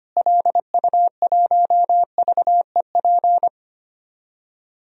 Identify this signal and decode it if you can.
Morse code, heard as a keyed tone, about 25 wpm, 700 Hz. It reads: LU1VEP
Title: DXCC-100 Call Signs - 6 Characters - Repeated 2x - 25wpm